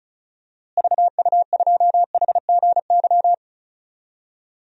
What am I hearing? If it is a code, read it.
Morse code, heard as a keyed tone, about 36 wpm, 700 Hz. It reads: VU2HGY
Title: DXCC-100 Call Signs - 6 Characters - Repeated 3x - 35wpm